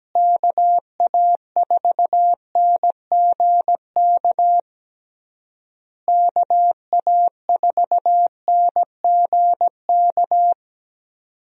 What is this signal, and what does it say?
Morse code, heard as a keyed tone, about 17 wpm, 700 Hz. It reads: KA4NGK KA4NGK
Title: USA Call Signs - 6 Characters - Repeated 3x - 17wpm